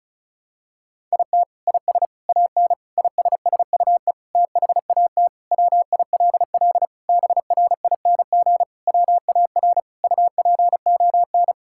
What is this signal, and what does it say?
Morse code, heard as a keyed tone, about 34 wpm, 700 Hz. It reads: IT IS AN ISSUE THAT WILL BRING WAR UPON
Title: Sentences from Top 1000 Words - Review Part 6 - 35wpm